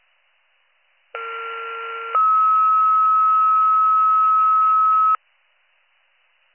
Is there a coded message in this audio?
Yes: two-tone page from 1.1 to 5.2 s, heard as a one-second tone, then a three-second tone, about 1370 then 1290 Hz.